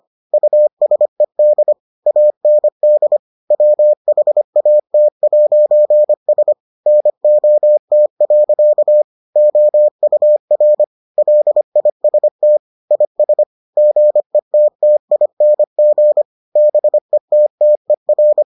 Morse code, heard as a keyed tone, about 25 wpm, 600 Hz. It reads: USED AND WHAT'S NOT. OUR LIST IS GETTING BETTER